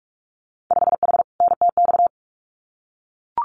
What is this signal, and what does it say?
Morse code, heard as a keyed tone, about 44 wpm, 700 Hz. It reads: 5H NTX
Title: Winter Field Day Exchanges Encore 45wpm